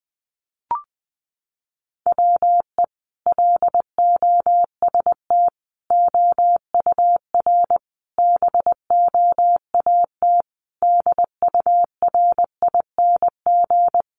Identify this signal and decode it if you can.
Morse code, heard as a keyed tone, about 20 wpm, 700 Hz. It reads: WE LOST OUR BOAT DURING
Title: Sentences from Top 1500 Words 20wpm